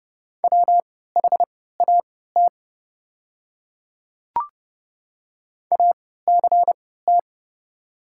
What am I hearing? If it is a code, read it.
Morse code, heard as a keyed tone, about 30 wpm, spaced as for 17 wpm, 700 Hz. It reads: WHAT ACT
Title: 3x Character Spacing - Top 200 Words - 30wpm